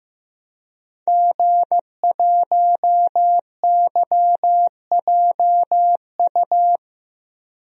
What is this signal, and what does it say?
Morse code, heard as a keyed tone, about 16 wpm, 700 Hz. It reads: G1YJU
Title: DXCC-100 Call Signs - 5 Characters - Repeated 3x - 15wpm